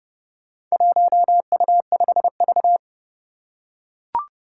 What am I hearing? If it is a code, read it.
Morse code, heard as a keyed tone, about 29 wpm, 700 Hz. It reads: 1U5V
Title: Four Challenging Letter-Number Combinations 30wpm